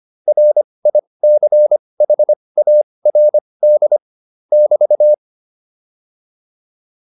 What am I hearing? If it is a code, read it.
Morse code, heard as a keyed tone, about 25 wpm, spaced as for 20 wpm, 600 Hz. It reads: RICHARD <BT>